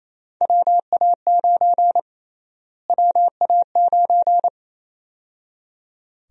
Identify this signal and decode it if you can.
Morse code, heard as a keyed tone, about 28 wpm, 700 Hz. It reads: WA9 WA9
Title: USA Call Sign Prefixes - Repeated 2x - 28wpm